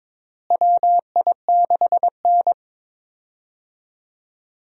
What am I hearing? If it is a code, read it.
Morse code, heard as a keyed tone, about 22 wpm, 700 Hz. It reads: WI6N
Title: USA Call Signs - 4 Characters - Repeated 3x - 22wpm